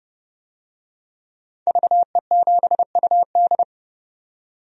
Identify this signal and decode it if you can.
Morse code, heard as a keyed tone, about 30 wpm, 700 Hz. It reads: VE7UD
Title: DXCC-100 Call Signs - 5 Characters - 30wpm